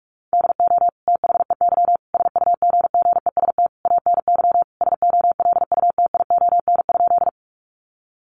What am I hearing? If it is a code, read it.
Morse code, heard as a keyed tone, about 44 wpm, 700 Hz. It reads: DO THEY SUGGEST ANY SOLUTION?
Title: Sentences from Top 1300 Words 45wpm